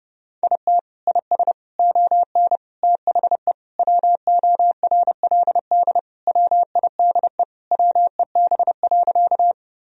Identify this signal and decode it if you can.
Morse code, heard as a keyed tone, about 32 wpm, 700 Hz. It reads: IT IS ON THE WORLD WIDE WEB.
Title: Sentences from Top 1000 Words - Review Part 4 - 30wpm